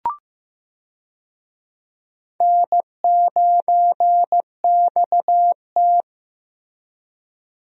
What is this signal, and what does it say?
Morse code, heard as a keyed tone, about 15 wpm, 700 Hz. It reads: N9XT
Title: USA Call Signs - 4 Characters - Repeated 3x - 15wpm